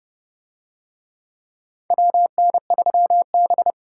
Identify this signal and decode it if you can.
Morse code, heard as a keyed tone, about 30 wpm, 700 Hz. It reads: WN3B